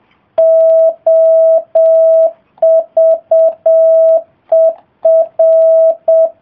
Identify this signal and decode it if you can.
Morse code, heard as a keyed tone, about 7 wpm, 650 Hz. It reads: OVER